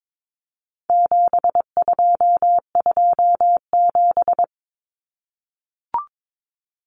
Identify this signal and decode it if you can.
Morse code, heard as a keyed tone, about 22 wpm, 700 Hz. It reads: 7227
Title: Common Cut Numbers - 4-Digit 22wpm